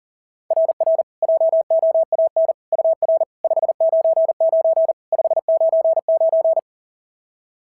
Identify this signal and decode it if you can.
Morse code, heard as a keyed tone, about 40 wpm, 690 Hz. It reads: RR JOAN UR 599 599
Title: SOTA QSO Practice 40wpm